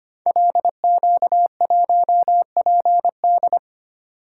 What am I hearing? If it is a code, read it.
Morse code, heard as a keyed tone, about 25 wpm, 700 Hz. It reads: LQ1PD
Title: Code Groups - 5 Letter:Numbers - 25wpm